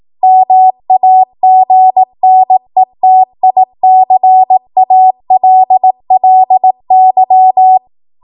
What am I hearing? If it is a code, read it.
Morse code, heard as a keyed tone, about 18 wpm, 750 Hz. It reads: MAGNETICALLY
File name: Archive / 18 WPM / 18WPM